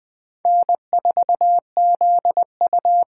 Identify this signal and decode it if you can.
Morse code, heard as a keyed tone, about 20 wpm, 700 Hz. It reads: N4ZU
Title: USA Call Signs - 5 Characters - 8x Character Spacing - 20wpm